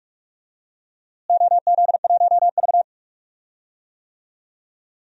Morse code, heard as a keyed tone, about 46 wpm, 700 Hz. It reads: OZ1V
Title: DXCC-100 Call Signs - 4 Characters - Repeated 2x - 45wpm